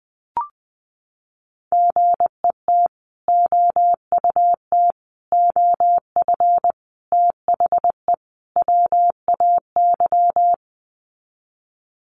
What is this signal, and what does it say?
Morse code, heard as a keyed tone, about 20 wpm, 700 Hz. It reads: GET OUT OF THE WAY
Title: Sets of 5 Words Encore 20wpm